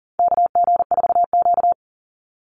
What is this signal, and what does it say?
Morse code, heard as a keyed tone, about 40 wpm, 700 Hz. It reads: KG4Q
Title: Call Signs - 40wpm